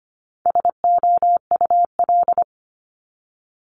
Morse code, heard as a keyed tone, about 25 wpm, 700 Hz. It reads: SOUL
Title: Multi-letter Phonemes Repeated 3x 25wpm